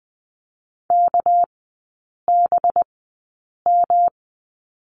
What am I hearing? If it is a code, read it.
Morse code, heard as a keyed tone, about 20 wpm, spaced as for 8 wpm, 700 Hz. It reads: KBM
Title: 205 Lesson 33e - Letter j - Three Characters 20wpm